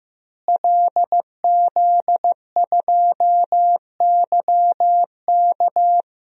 Morse code, heard as a keyed tone, about 15 wpm, 700 Hz. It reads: LZ2YK
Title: DXCC-100 Call Signs - 5 Characters - Repeated 2x - 15wpm